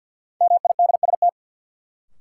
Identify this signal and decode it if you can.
Morse code, heard as a keyed tone, about 51 wpm, 700 Hz. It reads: MIDST
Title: Sets of 4 Words - New Word Drill - 50wpm